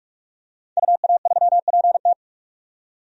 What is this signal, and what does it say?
Morse code, heard as a keyed tone, about 46 wpm, 700 Hz. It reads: UA3PT
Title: DXCC-100 Call Signs - 5 Characters - 45wpm